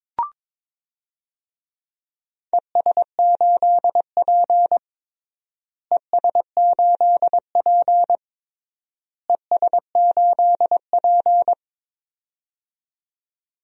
Morse code, heard as a keyed tone, about 22 wpm, 700 Hz. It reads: ES8P ES8P ES8P
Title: DXCC-100 Call Signs - 4 Characters - Repeated 3x - 22wpm